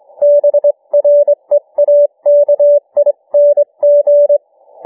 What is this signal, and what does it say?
Morse code, heard as a keyed tone, about 21 wpm, 590 Hz. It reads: BREAKING
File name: cw / ssn 40M